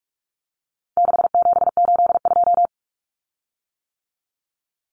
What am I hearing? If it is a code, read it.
Morse code, heard as a keyed tone, about 44 wpm, 700 Hz. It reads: 6782
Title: Common Cut Numbers - 4-Digit 45wpm